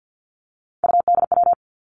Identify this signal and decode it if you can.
Morse code, heard as a keyed tone, about 50 wpm, 700 Hz. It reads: UDW
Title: Three Challenging Letter-Number Combinations 50wpm